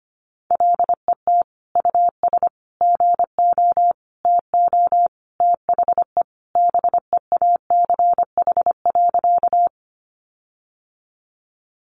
Morse code, heard as a keyed tone, about 25 wpm, 700 Hz. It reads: LET US GO TO THE BEACH.
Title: Sentences from Top 1500 Words 25wpm